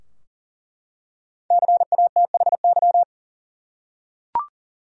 Morse code, heard as a keyed tone, about 40 wpm, 700 Hz. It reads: CATHY
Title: USA Common Names - 40wpm